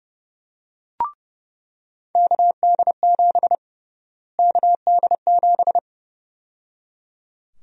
Morse code, heard as a keyed tone, about 30 wpm, 700 Hz. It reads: KD7 KD7
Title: USA Call Sign Prefixes - Repeated 2x - 30wpm